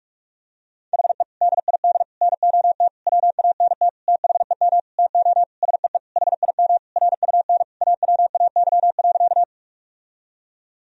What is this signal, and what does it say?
Morse code, heard as a keyed tone, about 46 wpm, 700 Hz. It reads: HE DID NOT WANT THEM TO SEE HIM RUN AWAY.
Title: Sentences from Top 1000 Words - Review Part 1 - 45wpm